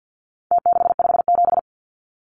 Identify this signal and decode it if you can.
Morse code, heard as a keyed tone, about 50 wpm, 700 Hz. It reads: T657
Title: Numbers - Contest Style - 4-Digit 50wpm